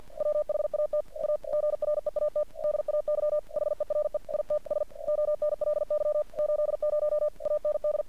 Morse code, heard as a keyed tone, about 50 wpm, 600 Hz. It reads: WHAT A GREAT DAY HERE ITS ONLY 80 AND
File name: archived / CRSnet